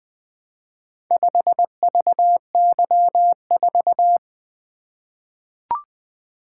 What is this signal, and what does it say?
Morse code, heard as a keyed tone, about 20 wpm, 700 Hz. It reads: 5VY4
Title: Four Letter-Number Combinations - 20wpm